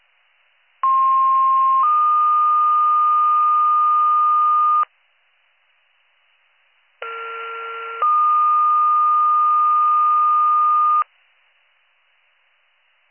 Two-tone page, heard as a one-second tone, then a three-second tone, about 1060 then 1220 Hz, about 1370 then 1220 Hz. It is heard from 0.8 to 11.0 s.